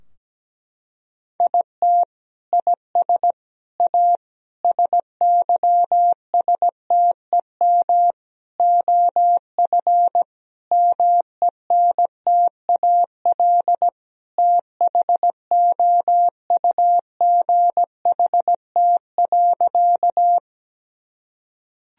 Morse code, heard as a keyed tone, about 17 wpm, 700 Hz. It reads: IT IS A SYSTEM OF MENTAL THOUGHT.
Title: Sentences from Top 1000 Words - Review Part 4 - 17wpm